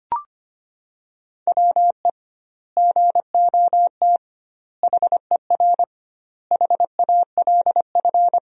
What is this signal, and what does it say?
Morse code, heard as a keyed tone, about 25 wpm, 700 Hz. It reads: WE GOT HER HALF
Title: 2x Word-Spacing - Sentences from Top 400 Words - 25wpm